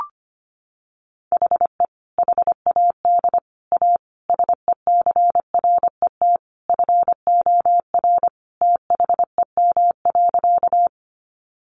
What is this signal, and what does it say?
Morse code, heard as a keyed tone, about 25 wpm, 700 Hz. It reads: HE HAD A SECRET FOR THEM.
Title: Sentences from Top 1200 Words 25wpm